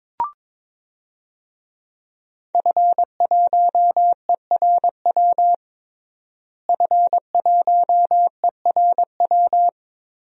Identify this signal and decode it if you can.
Morse code, heard as a keyed tone, about 22 wpm, 700 Hz. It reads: F1ERW F1ERW
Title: DXCC-100 Call Signs - 5 Characters - Repeated 2x - 22wpm